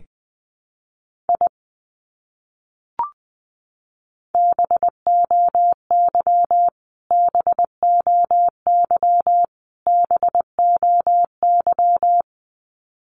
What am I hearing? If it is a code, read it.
Morse code, heard as a keyed tone, about 20 wpm, 700 Hz. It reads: I BOY BOY BOY
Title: Top 200 Words Repeated 20wpm